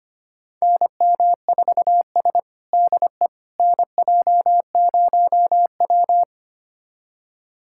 Morse code, heard as a keyed tone, about 25 wpm, 700 Hz. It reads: NM4S DE NJ0W
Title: USA Callsign DE Callsign - 4 Characters - 25wpm